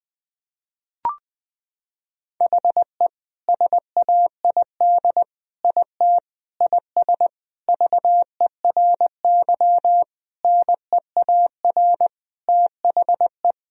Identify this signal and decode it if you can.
Morse code, heard as a keyed tone, about 21 wpm, 700 Hz. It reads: HE SAID IT IS VERY NEAR THE
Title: Sentences from Top 1000 Words - Review Part 2 - 20wpm